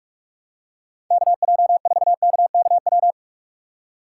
Morse code, heard as a keyed tone, about 43 wpm, 700 Hz. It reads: KJ4KKW
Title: USA Call Signs - 6 Characters - 4x Character Spacing - 45wpm